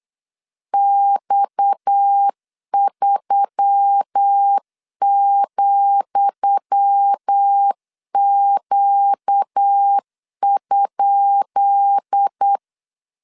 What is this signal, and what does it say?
Morse code, heard as a keyed tone, about 8 wpm, 800 Hz. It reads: X3,Q?
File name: LEZIONI / 10 SIGNS / 10D COMA 8wpm 3749